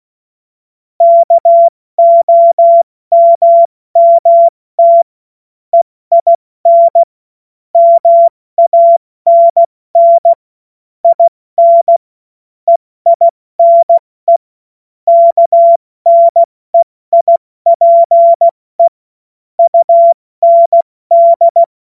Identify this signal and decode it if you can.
Morse code, heard as a keyed tone, about 16 wpm, 680 Hz. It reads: KOMMT EIN MANN IN EINE KNEIPE UND
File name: Morsekurs / Witze / Witz 11. Mai 2021, 70 F80